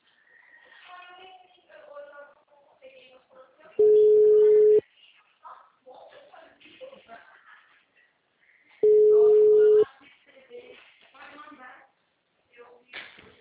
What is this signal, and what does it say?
Ringback tone (Europe, 425 Hz), with speech.